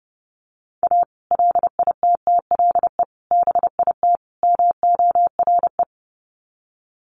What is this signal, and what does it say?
Morse code, heard as a keyed tone, about 30 wpm, 700 Hz. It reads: A LITTLE BIT MORE
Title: Sets of 4 Words 30wpm